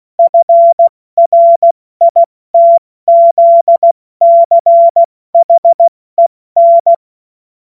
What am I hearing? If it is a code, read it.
Morse code, heard as a keyed tone, about 16 wpm, 680 Hz. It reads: FRITZCHEN
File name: Morsekurs / Witze / Witz 15. Juni 2021, 70 F80